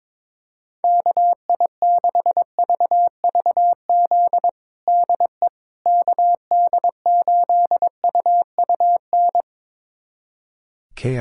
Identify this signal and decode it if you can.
Morse code, heard as a keyed tone, about 22 wpm, 700 Hz. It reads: KI6VVZ DE KD8UUN
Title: USA Callsign DE Callsign - 6 Characters - 22wpm